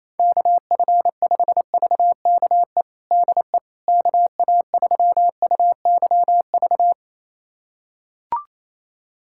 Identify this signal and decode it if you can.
Morse code, heard as a keyed tone, about 28 wpm, 700 Hz. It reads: KF5VKE DE KA3UYV
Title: USA Callsign DE Callsign - 6 Characters - 28wpm